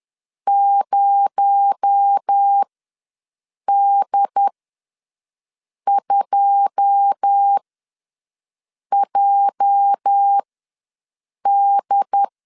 Morse code, heard as a keyed tone, about 10 wpm, spaced as for 5 wpm, 800 Hz. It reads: 0D2JD
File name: LEZIONI / 9 2867 ws 10wpm 4215